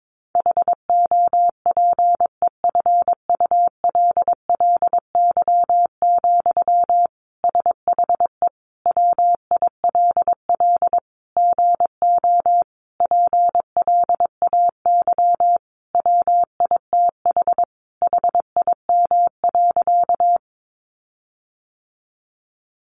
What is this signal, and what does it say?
Morse code, heard as a keyed tone, about 22 wpm, 700 Hz. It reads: HOPEFULLY, SHE WILL GO PLAY WITH HIM.
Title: Sentences from Top 1500 Words 22wpm